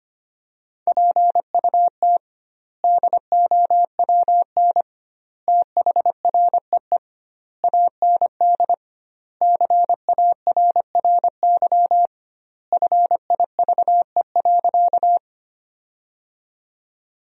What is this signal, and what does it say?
Morse code, heard as a keyed tone, about 25 wpm, 700 Hz. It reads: PUT DOWN THREE AND CARRY FIVE.
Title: 2x Word-Spacing - Sentences from Top 400 Words - 25wpm